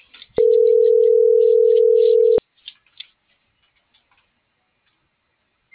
Ringback tone (North America), 440 + 480 Hz.